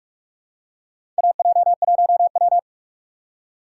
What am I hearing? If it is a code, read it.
Morse code, heard as a keyed tone, about 48 wpm, 700 Hz. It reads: AJ1W
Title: USA Call Signs - 4 Characters - 2x Character Spacing - 45wpm